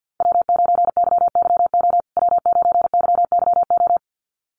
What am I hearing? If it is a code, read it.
Morse code, heard as a keyed tone, about 48 wpm, 700 Hz. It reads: W9YYO W9YYO
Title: USA Call Signs Repeated 2x 50wpm